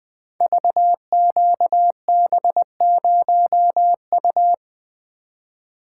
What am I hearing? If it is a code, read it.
Morse code, heard as a keyed tone, about 20 wpm, 700 Hz. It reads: VQB0U
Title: Code Groups - 5 Letter:Numbers - 20wpm